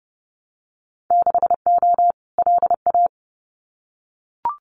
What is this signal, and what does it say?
Morse code, heard as a keyed tone, about 30 wpm, 700 Hz. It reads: 6O LA
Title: Winter Field Day Exchanges Encore 30wpm